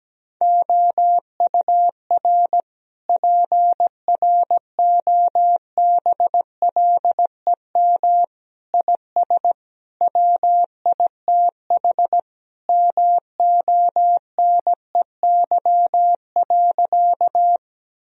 Morse code, heard as a keyed tone, about 17 wpm, 700 Hz. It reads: OUR PROBLEM IS WITH MONEY.
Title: Sentences from Top 1000 Words - Review Part 2 - 17wpm